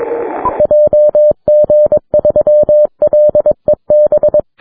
Morse code, heard as a keyed tone, about 22 wpm, 600 Hz. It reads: JG3LEB
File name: wp / LA8AJA heard by JG3LEB on 6m